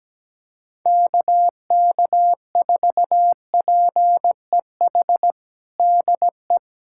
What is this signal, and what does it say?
Morse code, heard as a keyed tone, about 17 wpm, 700 Hz. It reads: KK4PEH DE
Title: USA Callsign DE Callsign - 6 Characters - 17wpm